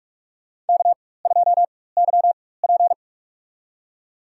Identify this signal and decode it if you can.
Morse code, heard as a keyed tone, about 46 wpm, spaced as for 21 wpm, 700 Hz. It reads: K2YP
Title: USA Call Signs - 4 Characters - 4x Character Spacing - 45wpm